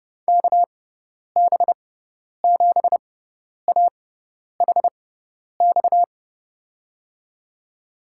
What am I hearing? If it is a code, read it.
Morse code, heard as a keyed tone, about 29 wpm, spaced as for 10 wpm, 700 Hz. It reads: KB7AHX
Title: USA Call Signs - 6x Character Spacing - 30wpm